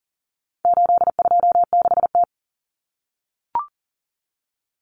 Morse code, heard as a keyed tone, about 40 wpm, 700 Hz. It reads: 826T
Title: Common Cut Numbers - 4-Digit 40wpm